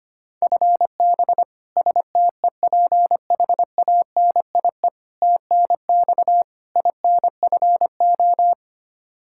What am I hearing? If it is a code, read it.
Morse code, heard as a keyed tone, about 25 wpm, 700 Hz. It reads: FB STEPHANIE TNX INFO
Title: Practice QSO - Exchange 3 - More Info - 25wpm